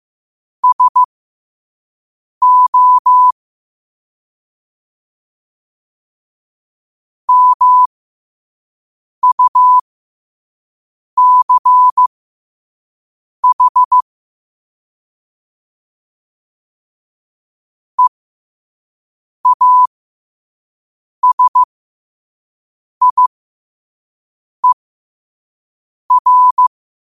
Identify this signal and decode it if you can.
Morse code, heard as a keyed tone, about 15 wpm, spaced as for 5 wpm, 1000 Hz. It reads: SO MUCH EASIER
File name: qotdfiles / 5wpm